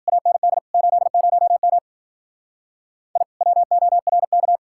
Morse code, heard as a keyed tone, about 48 wpm, 700 Hz. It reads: AND 80M I WORK
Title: Speed-Racing Common QSO Phrases - 40wpm